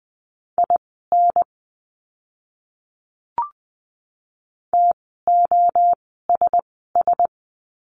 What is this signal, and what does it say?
Morse code, heard as a keyed tone, about 20 wpm, spaced as for 15 wpm, 700 Hz. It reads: IN TOSS
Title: 036 Lesson 7f - Letter s - Words 20wpm